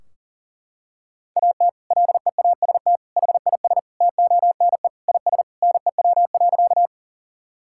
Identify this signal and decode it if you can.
Morse code, heard as a keyed tone, about 40 wpm, 700 Hz. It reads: AT LEAST HIS TONE IS NEW.
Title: Sentences from Top 1600 Words 40wpm